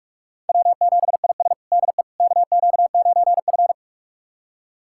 Morse code, heard as a keyed tone, about 46 wpm, 700 Hz. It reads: W7IS DE KQ9F
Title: USA Callsign DE Callsign - 4 Characters - 45wpm